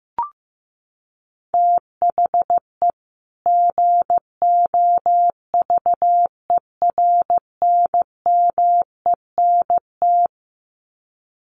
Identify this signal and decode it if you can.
Morse code, heard as a keyed tone, about 15 wpm, 700 Hz. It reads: THE GOVERNMENT
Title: Sets of 2 Words 15wpm